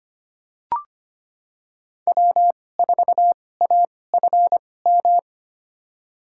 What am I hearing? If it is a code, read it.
Morse code, heard as a keyed tone, about 25 wpm, spaced as for 18 wpm, 700 Hz. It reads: W4AFM
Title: USA Call Signs - 5 Characters - 2x Character Spacing - 25wpm